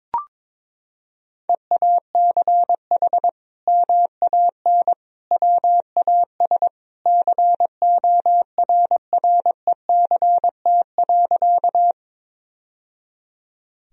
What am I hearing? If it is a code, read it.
Morse code, heard as a keyed tone, about 22 wpm, 700 Hz. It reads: EACH MAN WAS CORRECT.
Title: Sentences from Top 1000 Words - Review Part 2 - 22wpm